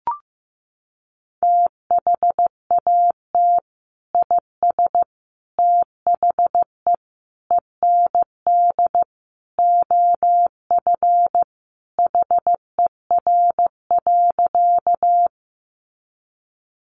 Morse code, heard as a keyed tone, about 15 wpm, 700 Hz. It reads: THAT IS THE END OF HER.
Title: Sentences from Top 200 Words 15wpm